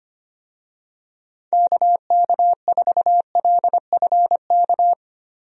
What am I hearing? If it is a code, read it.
Morse code, heard as a keyed tone, about 25 wpm, 700 Hz. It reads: KK4LFK
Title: USA Call Signs - 6 Characters - Repeated 3x - 25wpm